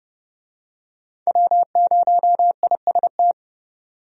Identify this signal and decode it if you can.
Morse code, heard as a keyed tone, about 30 wpm, 700 Hz. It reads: W0IST
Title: USA Call Signs - 5 Characters - 30wpm